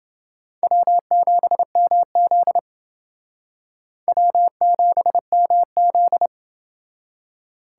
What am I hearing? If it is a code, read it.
Morse code, heard as a keyed tone, about 29 wpm, 700 Hz. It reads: W7MZ W7MZ